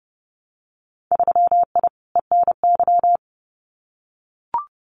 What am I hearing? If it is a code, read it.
Morse code, heard as a keyed tone, about 30 wpm, 700 Hz. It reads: 3I ENY
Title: Winter Field Day Exchanges Encore 30wpm